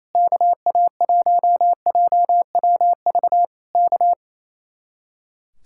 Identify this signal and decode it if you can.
Morse code, heard as a keyed tone, about 26 wpm, 700 Hz. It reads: KA1JWV K
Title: Speed-Racing Common QSO Phrases - 28wpm